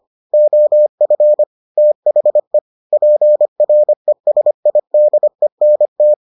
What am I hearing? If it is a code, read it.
Morse code, heard as a keyed tone, about 25 wpm, 600 Hz. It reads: OF THE PRESIDENT